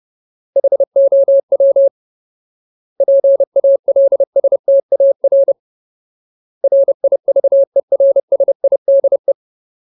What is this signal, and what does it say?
Morse code, heard as a keyed tone, about 30 wpm, 550 Hz. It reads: HOW PALSTAR RIVERSIDE
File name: Ultimate Word List / 30WPM Ultimate